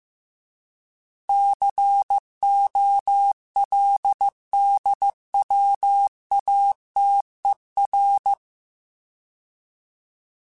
Morse code, heard as a keyed tone, about 15 wpm, 800 Hz. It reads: COLDWATER
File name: fromSMs / FromSMs mp3s / Cities@13